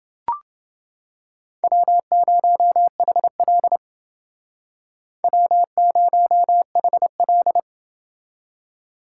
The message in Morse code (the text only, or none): W0HL W0HL